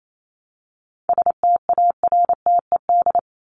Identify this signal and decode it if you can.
Morse code, heard as a keyed tone, about 28 wpm, 700 Hz. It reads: STARTED
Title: ED Suffix - Top 100 Words with Base Words 28wpm